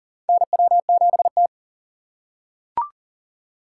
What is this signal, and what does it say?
Morse code, heard as a keyed tone, about 40 wpm, 700 Hz. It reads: NW7T
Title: USA Call Signs - 4 Characters - 3x Character Spacing - 40wpm